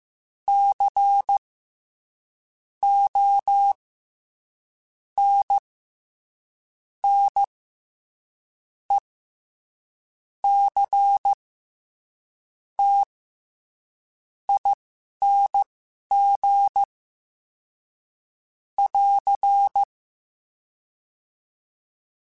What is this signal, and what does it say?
Morse code, heard as a keyed tone, about 15 wpm, spaced as for 11 wpm, 800 Hz. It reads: C O N N E C T ING <AR>
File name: Special INGs-mp3s / Special INGs-mp3s / SlowToFaster ing @10